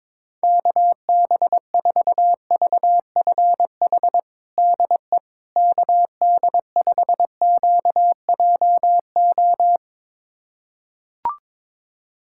Morse code, heard as a keyed tone, about 21 wpm, 700 Hz. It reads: KB4VFH DE KD5QJO
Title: USA Callsign DE Callsign 22wpm